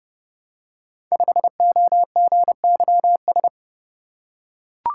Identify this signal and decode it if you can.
Morse code, heard as a keyed tone, about 30 wpm, 700 Hz. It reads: 5OGYS
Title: Code Groups - 5 Letter:Numbers - 30wpm